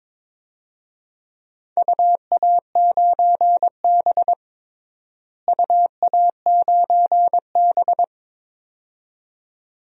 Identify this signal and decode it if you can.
Morse code, heard as a keyed tone, about 22 wpm, 700 Hz. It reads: UA9B UA9B
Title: DXCC-100 Call Signs - 4 Characters - Repeated 2x - 22wpm